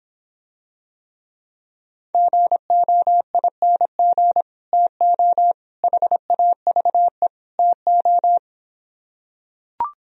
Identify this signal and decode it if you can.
Morse code, heard as a keyed tone, about 26 wpm, 700 Hz. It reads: GOING TO HAVE TO